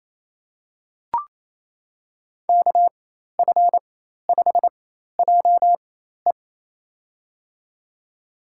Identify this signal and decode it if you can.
Morse code, heard as a keyed tone, about 28 wpm, spaced as for 13 wpm, 700 Hz. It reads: KF5JE
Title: USA Call Signs - 5 Characters - 4x Character Spacing - 28wpm